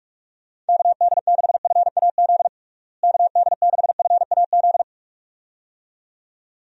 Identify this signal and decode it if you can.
Morse code, heard as a keyed tone, about 46 wpm, 700 Hz. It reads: KD6FAZ KD6FAZ
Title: USA Call Signs - 6 Characters - Repeated 2x - 45wpm